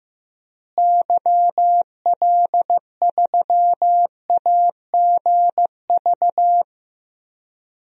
Morse code, heard as a keyed tone, about 15 wpm, 700 Hz. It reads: YL3AGV
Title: DXCC-100 Call Signs - 6 Characters - Repeated 2x - 15wpm